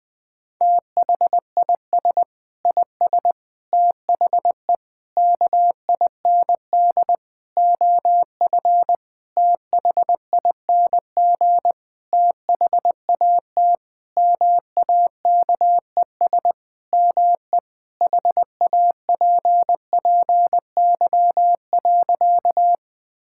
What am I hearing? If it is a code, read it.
Morse code, heard as a keyed tone, about 20 wpm, 700 Hz. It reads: THIS IS THE KIND OF THING THAT MAKES ME HAPPY.
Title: Sentences with 7-Word Phrases - 20wpm